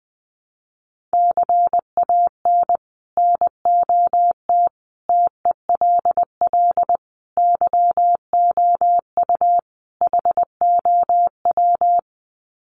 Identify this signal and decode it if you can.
Morse code, heard as a keyed tone, about 20 wpm, 700 Hz. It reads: CAN NOT TELL YOU HOW
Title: Sets of 5 Words Encore2 20wpm